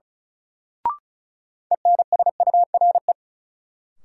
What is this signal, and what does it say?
Morse code, heard as a keyed tone, about 34 wpm, 700 Hz. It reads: ENSURE
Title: Sentences from Top 1100 Words - New Word Drill Alt - 35wpm